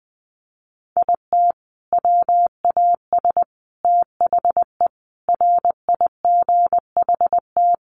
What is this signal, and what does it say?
Morse code, heard as a keyed tone, about 20 wpm, 700 Hz. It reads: IT WAS THE RIGHT
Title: Sentences from Top 1400 Words 20wpm